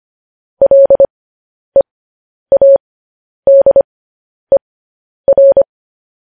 Morse code, heard as a keyed tone, about 25 wpm, spaced as for 10 wpm, 550 Hz. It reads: LEADER
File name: podcast / News headlines-1x-25x10